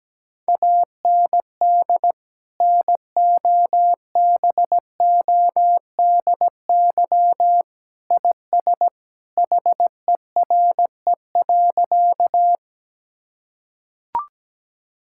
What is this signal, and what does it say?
Morse code, heard as a keyed tone, about 17 wpm, 700 Hz. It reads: AND NOBODY IS HERE.
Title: Sentences from Top 1600 Words 17wpm